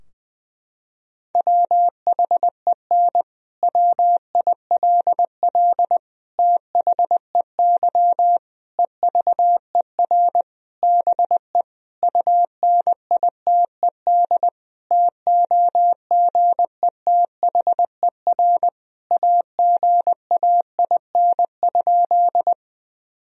Morse code, heard as a keyed tone, about 20 wpm, 700 Hz. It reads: WHEN WILL THEY EVER BE UNITED TOGETHER AGAIN?
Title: Sentences from Top 1000 Words - Review Part 4 - 20wpm